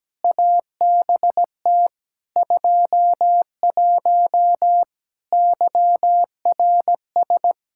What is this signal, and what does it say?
Morse code, heard as a keyed tone, about 17 wpm, 700 Hz. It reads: ABT 21 YRS
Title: Practice QSO - Exchange 3 - More Info - 17wpm